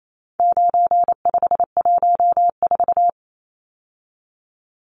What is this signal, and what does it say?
Morse code, heard as a keyed tone, about 28 wpm, 700 Hz. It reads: 9514
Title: Numbers - 4-Digit 28wpm